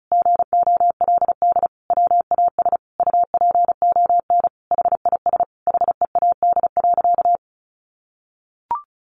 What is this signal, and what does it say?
Morse code, heard as a keyed tone, about 34 wpm, 700 Hz. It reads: GOLD WAS UPON HIS HEAD.
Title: Sentences from Top 1100 Words 35wpm